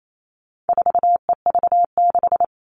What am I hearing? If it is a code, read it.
Morse code, heard as a keyed tone, about 28 wpm, 700 Hz. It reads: 4EV6
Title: All Cut Numbers - 4-Digit 28wpm